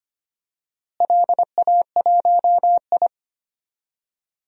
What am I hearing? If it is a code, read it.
Morse code, heard as a keyed tone, about 25 wpm, 700 Hz. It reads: LA1I
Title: DXCC-100 Call Signs - 4 Characters - Repeated 2x - 25wpm